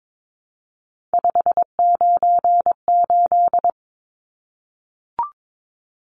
Morse code, heard as a keyed tone, about 22 wpm, 700 Hz. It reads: 598
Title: Numbers - 3-Digit 22wpm